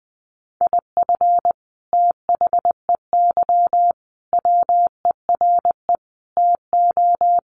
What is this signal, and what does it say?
Morse code, heard as a keyed tone, about 20 wpm, 700 Hz. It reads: IF THEY WERE TO
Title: Sentences from Top 1500 Words 20wpm